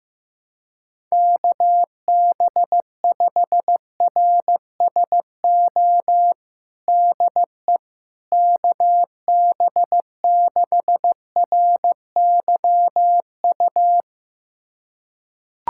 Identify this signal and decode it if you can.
Morse code, heard as a keyed tone, about 15 wpm, 700 Hz. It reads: KB5RSO DE KB6RYU
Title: USA Callsign DE Callsign - 6 Characters - 15wpm